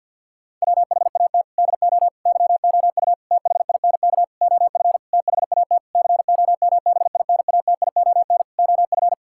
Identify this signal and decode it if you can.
Morse code, heard as a keyed tone, about 50 wpm, 700 Hz. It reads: WHAT DO YOU THINK OF THAT COMBINATION OF
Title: Sentences from Top 1000 Words - Review Part 6 - 50wpm